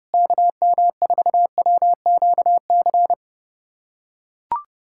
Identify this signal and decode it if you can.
Morse code, heard as a keyed tone, about 32 wpm, 700 Hz. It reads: KM4WQC
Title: USA Callsign DE Callsign - 6 Characters - 30wpm